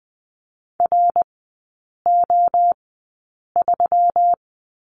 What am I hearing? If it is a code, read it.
Morse code, heard as a keyed tone, about 20 wpm, spaced as for 8 wpm, 700 Hz. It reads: RO3
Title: 138 Lesson 23e - Letter f - Three Characters 20wpm